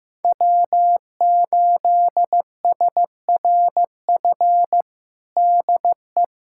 Morse code, heard as a keyed tone, about 15 wpm, 700 Hz. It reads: W8SRF DE
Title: USA Callsign DE Callsign - 5 Characters - 15wpm